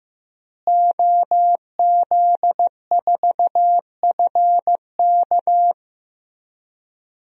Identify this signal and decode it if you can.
Morse code, heard as a keyed tone, about 16 wpm, 700 Hz. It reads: OZ4FK
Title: DXCC-100 Call Signs - 5 Characters - Repeated 2x - 15wpm